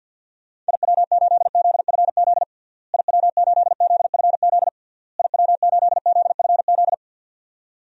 Morse code, heard as a keyed tone, about 48 wpm, 700 Hz. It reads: IW8ZFZ IW8ZFZ IW8ZFZ
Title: DXCC-100 Call Signs - 6 Characters - Repeated 3x - 50wpm